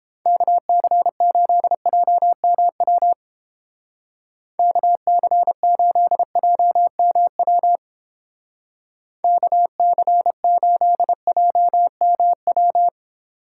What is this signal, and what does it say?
Morse code, heard as a keyed tone, about 30 wpm, 700 Hz. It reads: KC8JMW KC8JMW KC8JMW